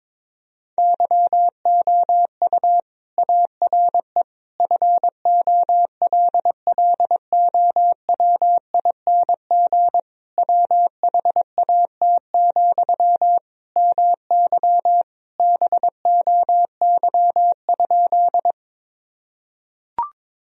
Morse code, heard as a keyed tone, about 22 wpm, 700 Hz. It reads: YOU ARE FOLLOWING WHAT, MY BOY?
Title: Sentences from Top 1000 Words - Review Part 4 - 22wpm